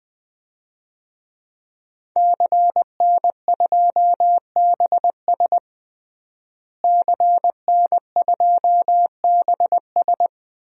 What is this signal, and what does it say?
Morse code, heard as a keyed tone, about 20 wpm, 700 Hz. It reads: CN2BS CN2BS
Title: DXCC-100 Call Signs - 5 Characters - Repeated 2x - 20wpm